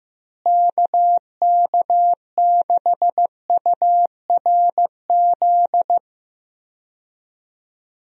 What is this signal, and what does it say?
Morse code, heard as a keyed tone, about 15 wpm, 700 Hz. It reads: KK6URZ
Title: USA Call Signs - 6 Characters - Repeated 3x - 15wpm